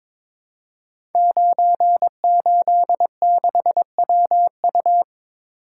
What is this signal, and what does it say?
Morse code, heard as a keyed tone, about 23 wpm, 690 Hz. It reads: 986WU
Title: Code Groups - 5 Letter:Numbers - 22wpm